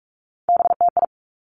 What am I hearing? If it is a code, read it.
Morse code, heard as a keyed tone, about 46 wpm, 700 Hz. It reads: BTI
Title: Three Challenging Letter-Number Combinations 45wpm